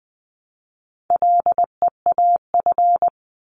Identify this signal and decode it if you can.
Morse code, heard as a keyed tone, about 20 wpm, 700 Hz. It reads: LEAF
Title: 260 Lesson 42e - Words 20-20wpm